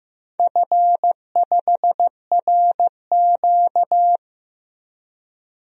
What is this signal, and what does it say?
Morse code, heard as a keyed tone, about 15 wpm, 700 Hz. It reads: F5RQ
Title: DXCC-100 Call Signs - 4 Characters - Repeated 2x - 15wpm